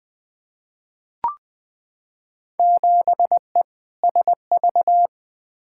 Morse code, heard as a keyed tone, about 20 wpm, 700 Hz. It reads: 7E SV
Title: Field Day Exchanges Encore - 20wpm